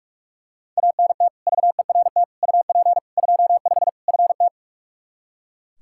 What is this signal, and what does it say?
Morse code, heard as a keyed tone, about 44 wpm, 700 Hz. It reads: ANT VERT UP 25 FT
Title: K1USN SST QSO Practice 45wpm